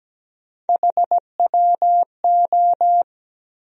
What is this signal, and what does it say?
Morse code, heard as a keyed tone, about 17 wpm, 700 Hz. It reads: HWO
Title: USA Call Sign Postfixes - 3 Letters 17wpm